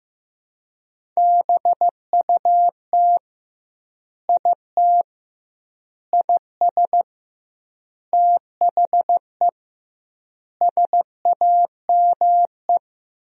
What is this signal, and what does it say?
Morse code, heard as a keyed tone, about 15 wpm, 700 Hz. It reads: BUT IT IS THE SAME
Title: 2x Word-Spacing - Sentences from Top 300 Words - 15wpm